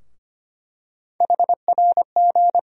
Morse code, heard as a keyed tone, about 25 wpm, 700 Hz. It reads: HRG
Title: USA Call Sign Postfixes - 3 Letters 25wpm